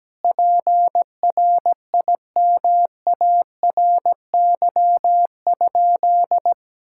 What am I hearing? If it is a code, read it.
Morse code, heard as a keyed tone, about 17 wpm, 700 Hz. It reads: PRIMARY?
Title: Sentences from Top 1000 Words - Review Part 5 - 17wpm